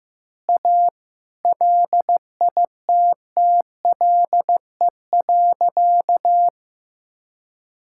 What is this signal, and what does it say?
Morse code, heard as a keyed tone, about 15 wpm, 700 Hz. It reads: A LITTLE.
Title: Sentences from Top 1000 Words - Review Part 5 - 15wpm